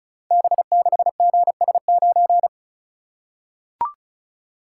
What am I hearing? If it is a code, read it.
Morse code, heard as a keyed tone, about 36 wpm, 700 Hz. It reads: B6GS9
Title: Code Groups - 5 Letter:Numbers - 35wpm